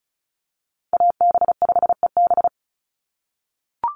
Morse code, heard as a keyed tone, about 34 wpm, 700 Hz. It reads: AB5EB
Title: Call Signs - 35wpm